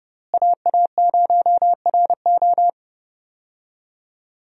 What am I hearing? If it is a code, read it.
Morse code, heard as a keyed tone, about 32 wpm, 700 Hz. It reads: AA0RO
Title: USA Call Signs - 5 Characters - Repeated 3x - 30wpm